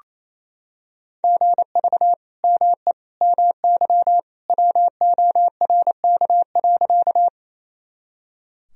Morse code, heard as a keyed tone, about 28 wpm, 700 Hz. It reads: GV ME MY WORK.
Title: Sentences from Top 200 Words and Abbreviations 28wpm